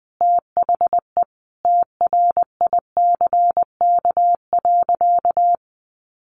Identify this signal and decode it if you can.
Morse code, heard as a keyed tone, about 20 wpm, 700 Hz. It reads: THE TRICK.
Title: Sentences from Top 1500 Words 20wpm